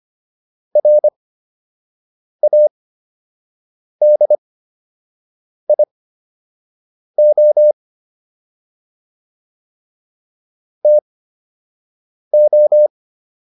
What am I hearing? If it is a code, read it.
Morse code, heard as a keyed tone, about 25 wpm, spaced as for 6 wpm, 600 Hz. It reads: RADIO TO